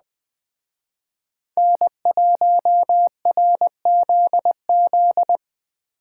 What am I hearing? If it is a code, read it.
Morse code, heard as a keyed tone, about 20 wpm, 700 Hz. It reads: N1RZZ
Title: Speed Racing - USA 1x3 Call Signs 17wpm